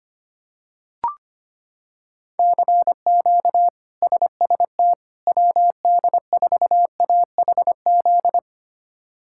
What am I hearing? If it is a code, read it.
Morse code, heard as a keyed tone, about 25 wpm, 700 Hz. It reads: CQ SST WD4AHZ
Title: K1USN SST QSO Practice - 25wpm